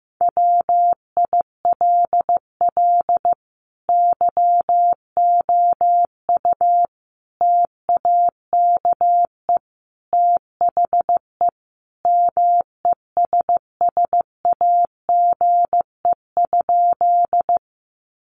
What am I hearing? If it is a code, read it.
Morse code, heard as a keyed tone, about 15 wpm, 700 Hz. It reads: WILL YOU TAKE THE MESSAGE?
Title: Sentences from Top 1100 Words 15wpm